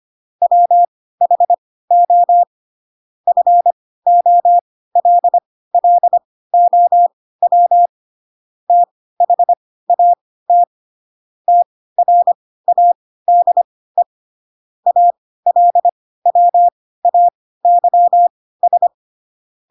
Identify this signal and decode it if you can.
Morse code, heard as a keyed tone, about 25 wpm, spaced as for 16 wpm, 700 Hz. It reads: WHO FOLLOW THAT TRADE ALWAYS
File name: Pinocchio / Pinocchio ch4 10 25 ik1hge